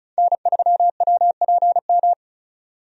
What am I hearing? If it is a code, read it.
Morse code, heard as a keyed tone, about 36 wpm, 700 Hz. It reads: N3WPM
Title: USA Call Signs - 5 Characters - 35wpm